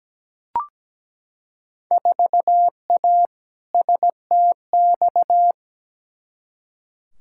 Morse code, heard as a keyed tone, about 17 wpm, 700 Hz. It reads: 4A STX
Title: Field Day Exchanges - 17wpm